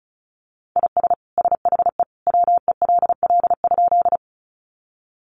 Morse code, heard as a keyed tone, about 34 wpm, 700 Hz. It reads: IS SHE WELL?
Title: Sentences from Top 200 Words 35wpm